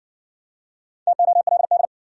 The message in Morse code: TOLD

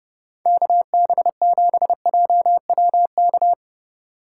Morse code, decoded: KB7JWK

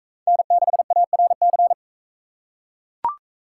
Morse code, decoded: N6ARC

